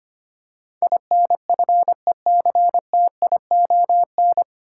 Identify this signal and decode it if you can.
Morse code, heard as a keyed tone, about 25 wpm, 700 Hz. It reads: INFECTION
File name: practice / 25wpm / 9 Letter Words - First 3 Letters Given - 25wpm